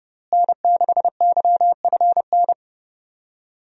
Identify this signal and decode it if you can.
Morse code, heard as a keyed tone, about 30 wpm, 700 Hz. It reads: N6YFN
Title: USA Call Signs - 5 Characters - Repeated 3x - 30wpm